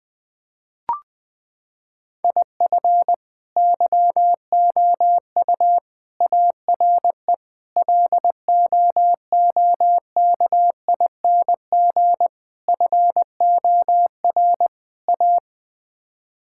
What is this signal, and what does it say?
Morse code, heard as a keyed tone, about 21 wpm, 700 Hz. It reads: IF YOU ARE LOOKING FOR A
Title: Sets of 6 Words - 20wpm